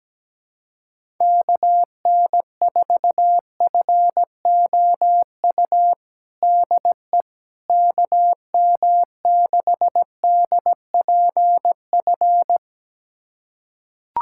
Morse code, heard as a keyed tone, about 17 wpm, 700 Hz. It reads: KN4FOU DE KM6DPF
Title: USA Callsign DE Callsign - 6 Characters - 17wpm